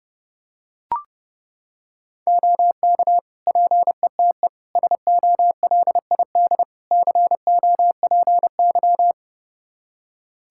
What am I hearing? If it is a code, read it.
Morse code, heard as a keyed tone, about 32 wpm, 700 Hz. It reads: OK PETE SOLID COPY
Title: Practice QSO - Exchange 3 - More Info - 30wpm